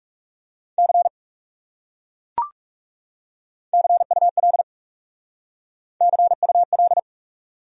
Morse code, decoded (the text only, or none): C CUL CUL